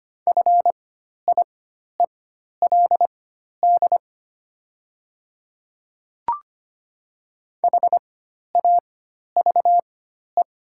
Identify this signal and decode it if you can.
Morse code, heard as a keyed tone, about 25 wpm, spaced as for 12 wpm, 700 Hz. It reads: FIELD HAVE
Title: 4x Character Spacing - Top 500 Words - 25wpm